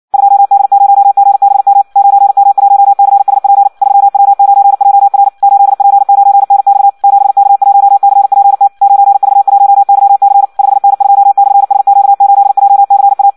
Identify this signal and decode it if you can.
Morse code, heard as a keyed tone, about 56 wpm, 800 Hz. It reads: 2D0GBM 9M1CSP 3W9JR 8P9NX 7K1CPT 9V1YC 5N2/SQ8JCA
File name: Audio practica CW / qrq-tx-58 2